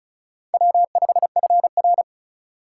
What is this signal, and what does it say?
Morse code, heard as a keyed tone, about 36 wpm, 700 Hz. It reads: W5FR
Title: USA Call Signs - 4 Characters - 3x Character Spacing - 35wpm